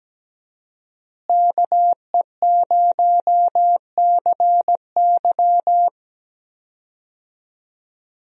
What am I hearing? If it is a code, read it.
Morse code, heard as a keyed tone, about 17 wpm, 700 Hz. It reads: KE0CY
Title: USA Call Signs - 5 Characters - Repeated 2x - 17wpm